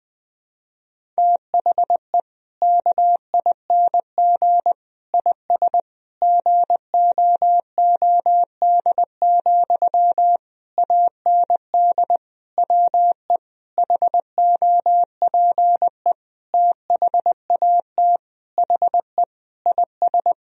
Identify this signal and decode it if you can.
Morse code, heard as a keyed tone, about 20 wpm, 700 Hz. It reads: THE KING IS GOOD, AND WE HOPE THAT HE IS
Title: Sentences from Top 1000 Words - Review Part 1 - 20wpm